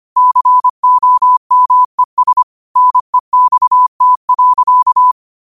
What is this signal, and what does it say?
Morse code, heard as a keyed tone, about 25 wpm, 1000 Hz. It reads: COMES NEXT.